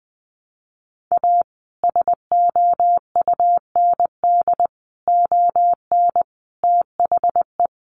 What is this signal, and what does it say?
Morse code, heard as a keyed tone, about 20 wpm, 700 Hz. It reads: A SOUND ON THE
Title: A Sound Of Thunder 20wpm